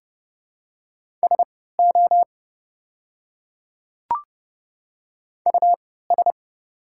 Morse code, heard as a keyed tone, about 30 wpm, spaced as for 17 wpm, 700 Hz. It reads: SO US
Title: 3x Character Spacing - Top 500 Words - 30wpm